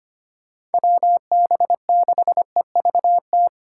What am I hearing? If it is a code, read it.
Morse code, heard as a keyed tone, about 30 wpm, 700 Hz. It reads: WB6EVT
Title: Speed Racing - USA 2x3 Call Signs 25wpm